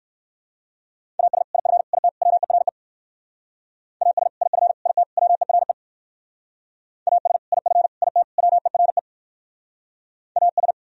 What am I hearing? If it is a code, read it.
Morse code, heard as a keyed tone, about 53 wpm, 700 Hz. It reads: AS IF IT WERE AS IF IT WERE AS IF IT WERE AS
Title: Speed-Racing - Sets of 4 Words 45wpm